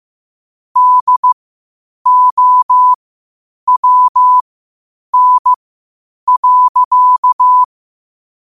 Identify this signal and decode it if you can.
Morse code, heard as a keyed tone, about 15 wpm, spaced as for 8 wpm, 1000 Hz. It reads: DOWN.
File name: qotdfiles / 8wpm